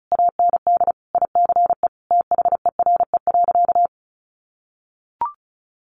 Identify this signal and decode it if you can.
Morse code, heard as a keyed tone, about 34 wpm, 700 Hz. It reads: AND ICE THERE.
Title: Sentences from Top 1000 Words 35wpm